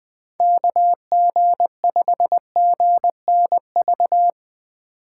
Morse code, heard as a keyed tone, about 21 wpm, 700 Hz. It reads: KG5GNV
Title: USA Call Signs - 6 Characters - 3x Character Spacing - 20wpm